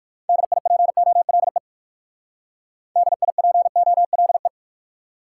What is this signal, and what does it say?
Morse code, heard as a keyed tone, about 56 wpm, 700 Hz. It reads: DIPOLE DIPOLE
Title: Speed-Racing Top 180 QSO Elements 45wpm